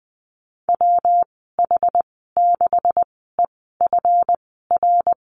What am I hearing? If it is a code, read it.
Morse code, heard as a keyed tone, about 20 wpm, spaced as for 14 wpm, 700 Hz. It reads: WH6EFR
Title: 161 Lesson 26g - Letter g - Call Signs 20wpm